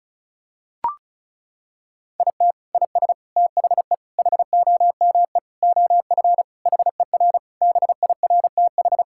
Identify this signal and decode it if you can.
Morse code, heard as a keyed tone, about 36 wpm, 700 Hz. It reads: IT IS THE HOME OF HER BIRTH
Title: Sentences from Top 1000 Words - Review Part 5 - 35wpm